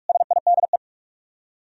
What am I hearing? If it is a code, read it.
Morse code, heard as a keyed tone, about 64 wpm, spaced as for 54 wpm, 700 Hz. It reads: SIDE
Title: Speed-Racing - Sets of 3 Words 45wpm